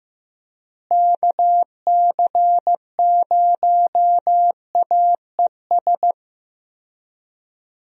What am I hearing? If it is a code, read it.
Morse code, heard as a keyed tone, about 15 wpm, 700 Hz. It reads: KC0AES
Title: USA Call Signs - 6 Characters - Repeated 3x - 15wpm